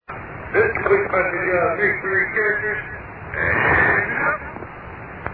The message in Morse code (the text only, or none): none